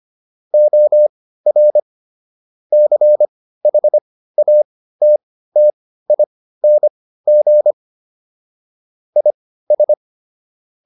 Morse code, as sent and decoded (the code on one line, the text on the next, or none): --- .-. / -.-. .... .- - - .. -. --. / .. ...
OR CHATTING IS